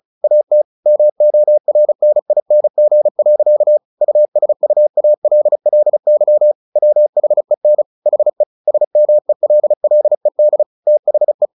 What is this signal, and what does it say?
Morse code, heard as a keyed tone, about 34 wpm, 600 Hz. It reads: AT MORNING. USUALLY WHEN HE SMELLED THE
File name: the old man and the sea / the old man and the sea 1 35wpm